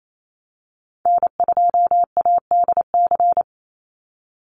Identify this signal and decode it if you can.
Morse code, heard as a keyed tone, about 28 wpm, 700 Hz. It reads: N2ADC
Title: Call Signs Repeated 3x - 28wpm